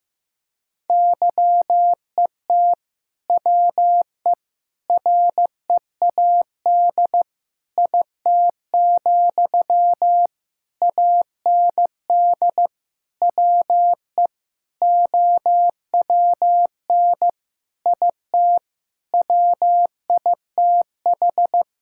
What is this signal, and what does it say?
Morse code, heard as a keyed tone, about 15 wpm, 700 Hz. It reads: YET WE READ IT, AND WE OWN IT WITH
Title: Sentences from Top 1600 Words 15wpm